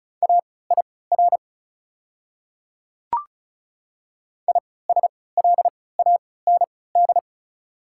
Morse code, heard as a keyed tone, about 34 wpm, spaced as for 20 wpm, 690 Hz. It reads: AIR ISLAND
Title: 3x Character Spacing - Top 500 Words - 35wpm